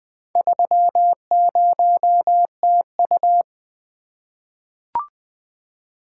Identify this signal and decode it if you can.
Morse code, heard as a keyed tone, about 20 wpm, 700 Hz. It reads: 30TU
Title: Four Letter-Number Combinations - 20wpm